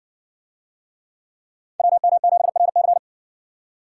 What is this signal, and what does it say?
Morse code, heard as a keyed tone, about 57 wpm, 700 Hz. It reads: WM7RC